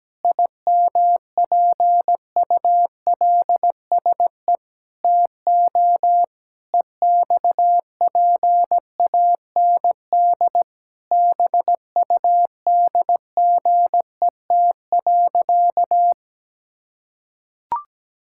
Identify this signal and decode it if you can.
Morse code, heard as a keyed tone, about 17 wpm, 700 Hz. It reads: IMPULSE TO EXPAND BUDGET.
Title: Pangram Challenge v2 - 17wpm